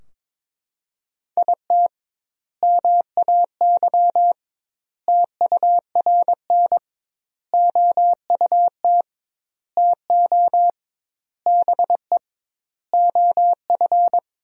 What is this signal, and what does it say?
Morse code, heard as a keyed tone, about 22 wpm, 700 Hz. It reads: IT MAY TURN OUT TO BE OF
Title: 2x Word-Spacing - Sentences from Top 400 Words - 22wpm